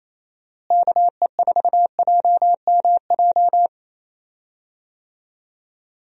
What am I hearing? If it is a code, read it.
Morse code, heard as a keyed tone, about 28 wpm, 700 Hz. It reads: KE4JMJ
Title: USA Call Signs - 6 Characters - Repeated 2x - 28wpm